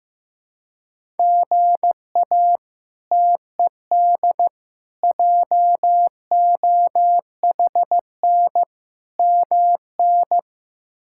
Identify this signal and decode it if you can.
Morse code, heard as a keyed tone, about 15 wpm, 700 Hz. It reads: GA TED JOHN MN
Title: K1USN SST QSO Practice - 15wpm